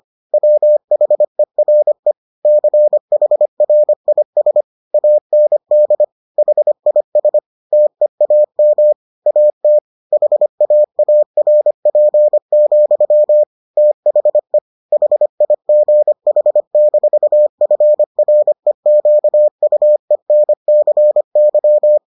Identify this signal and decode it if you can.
Morse code, heard as a keyed tone, about 25 wpm, 600 Hz. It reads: WHERE CHRIS AND HIS TEAM AT HAARP, THE HIGH-FREQUENCY